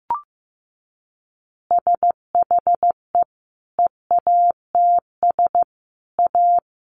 Morse code, heard as a keyed tone, about 15 wpm, 700 Hz. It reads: SHE EATS A
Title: Additional Easy Phrases 15wpm